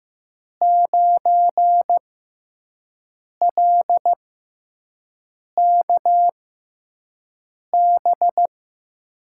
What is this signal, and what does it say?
Morse code, heard as a keyed tone, about 15 wpm, spaced as for 5 wpm, 700 Hz. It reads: 9LKB
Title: USA Call Signs - 6 Characters - 6x Character Spacing - 15wpm